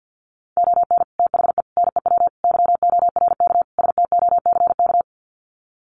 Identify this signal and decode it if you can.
Morse code, heard as a keyed tone, about 50 wpm, 700 Hz. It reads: ON THE NEW YORK STOCK
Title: Sets of 5 Words Encore 50wpm